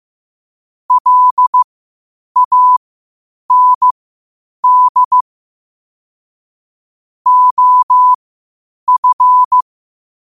Morse code, heard as a keyed tone, about 15 wpm, spaced as for 8 wpm, 1000 Hz. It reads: LAND OF
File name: qotdfiles / 8wpm